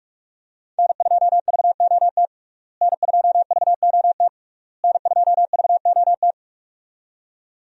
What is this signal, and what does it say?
Morse code, heard as a keyed tone, about 42 wpm, 700 Hz. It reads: N2VOT N2VOT N2VOT
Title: USA Call Signs - 5 Characters - Repeated 3x - 45wpm